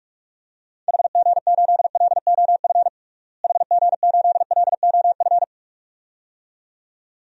Morse code, heard as a keyed tone, about 43 wpm, 700 Hz. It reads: HG8LOF HG8LOF
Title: DXCC-100 Call Signs - 6 Characters - Repeated 2x - 45wpm